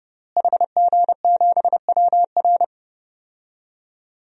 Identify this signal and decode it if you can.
Morse code, heard as a keyed tone, about 30 wpm, 700 Hz. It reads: HG7WR
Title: DXCC-100 Call Signs - 5 Characters - Repeated 2x - 30wpm